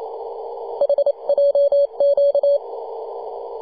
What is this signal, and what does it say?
Morse code, heard as a keyed tone, about 28 wpm, 570 Hz. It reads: HJQ
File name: 2015/May/15 2222z PW0F 21.026MHz CW